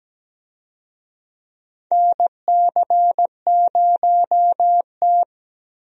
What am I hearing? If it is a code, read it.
Morse code, heard as a keyed tone, about 17 wpm, 700 Hz. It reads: NC0T
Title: USA Call Signs - 4 Characters - Repeated 3x - 17wpm